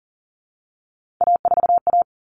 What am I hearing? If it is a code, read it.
Morse code, heard as a keyed tone, about 40 wpm, 700 Hz. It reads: A4A
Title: All Cut Numbers - 3-Digit 40wpm